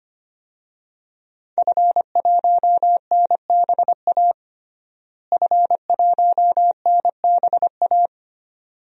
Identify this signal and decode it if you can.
Morse code, heard as a keyed tone, about 25 wpm, 700 Hz. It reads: F1NBA F1NBA
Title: DXCC-100 Call Signs - 5 Characters - Repeated 2x - 25wpm